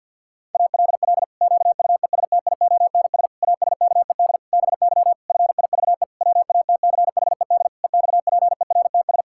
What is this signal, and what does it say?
Morse code, heard as a keyed tone, about 52 wpm, 700 Hz. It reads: ALL QUESTIONS ASKED BY FIVE WATCHED EXPERTS
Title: Pangram Challenge v2 - 50wpm